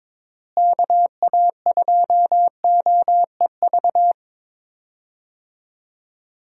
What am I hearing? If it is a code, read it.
Morse code, heard as a keyed tone, about 22 wpm, 700 Hz. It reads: KA2OEV
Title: USA Call Signs - 6 Characters - Repeated 2x - 22wpm